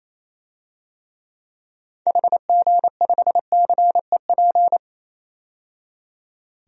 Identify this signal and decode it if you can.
Morse code, heard as a keyed tone, about 27 wpm, 700 Hz. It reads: HG5CEP
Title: DXCC-100 Call Signs - 6 Characters - 28wpm